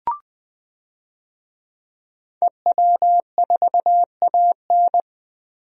Morse code, heard as a keyed tone, about 20 wpm, 700 Hz. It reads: EW4AN
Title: DXCC-100 Call Signs - 5 Characters - Repeated 3x - 20wpm